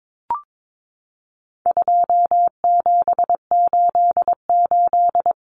Morse code, heard as a keyed tone, about 22 wpm, 700 Hz. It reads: 2788
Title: Common Cut Numbers - 4-Digit 22wpm